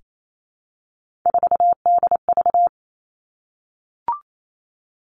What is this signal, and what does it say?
Morse code, heard as a keyed tone, about 28 wpm, 700 Hz. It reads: 4DV
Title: Three Challenging Letter-Number Combinations 28wpm